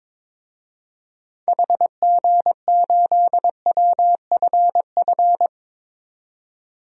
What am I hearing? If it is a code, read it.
Morse code, heard as a keyed tone, about 21 wpm, 700 Hz. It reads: HG8WFF
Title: DXCC-100 Call Signs - 6 Characters - 22wpm